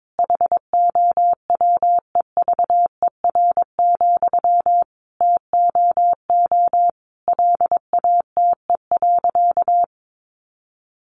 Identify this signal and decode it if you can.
Morse code, heard as a keyed tone, about 22 wpm, 700 Hz. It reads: HOWEVER, TOO LATE.
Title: Sentences from Top 600 Words 22wpm